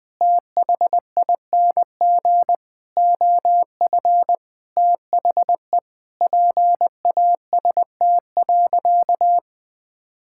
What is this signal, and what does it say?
Morse code, heard as a keyed tone, about 20 wpm, 700 Hz. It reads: THING OF THE PAST.
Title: Sentences with 6-Word Phrases - 20wpm